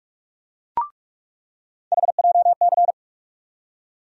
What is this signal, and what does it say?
Morse code, heard as a keyed tone, about 43 wpm, 700 Hz. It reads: HJC
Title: USA Call Sign Postfixes - 3 Letters 45wpm